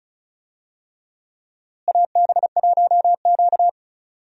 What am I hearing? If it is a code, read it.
Morse code, heard as a keyed tone, about 34 wpm, 700 Hz. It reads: AB1Q
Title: USA Call Signs - 4 Characters - Repeated 3x - 35wpm